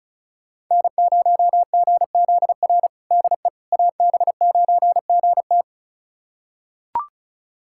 Morse code, heard as a keyed tone, about 36 wpm, 700 Hz. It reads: N0GZR DE AB9GT
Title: USA Callsign DE Callsign - 5 Characters - 35wpm